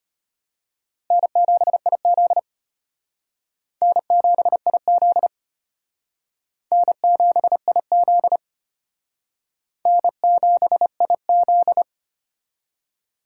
Morse code, heard as a keyed tone, about 33 wpm, 700 Hz. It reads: N7IZ N7IZ N7IZ N7IZ